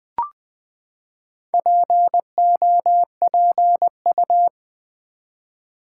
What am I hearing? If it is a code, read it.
Morse code, heard as a keyed tone, about 20 wpm, 700 Hz. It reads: POPU
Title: Build A Word - 4 Letters At A Time - 10-Letter Words - 20wpm